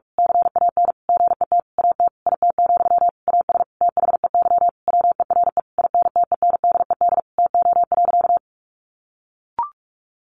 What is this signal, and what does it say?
Morse code, heard as a keyed tone, about 44 wpm, 700 Hz. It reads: CAN GET AT IT, AS THEY WERE INTENDED TO.
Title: A Sound Of Thunder 45wpm